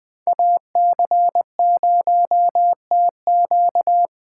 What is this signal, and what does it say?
Morse code, heard as a keyed tone, about 20 wpm, 700 Hz. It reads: AC0TQ
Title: Speed Racing - USA 2x2 Call Signs 17wpm